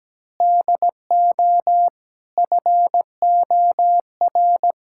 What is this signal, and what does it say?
Morse code, heard as a keyed tone, about 17 wpm, 700 Hz. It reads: DO FOR
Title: Sentences from Top 1600 Words 17wpm